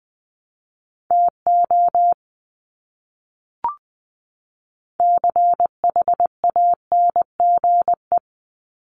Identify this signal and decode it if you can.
Morse code, heard as a keyed tone, about 20 wpm, 700 Hz. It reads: TO CHANGE
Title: Top 500 Words 20wpm